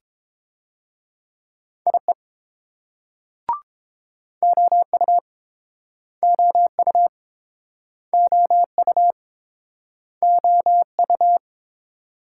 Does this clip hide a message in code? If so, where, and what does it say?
Yes: Morse code from 1.9 to 11.4 s, heard as a keyed tone, about 30 wpm, 700 Hz. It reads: IE OU OU OU OU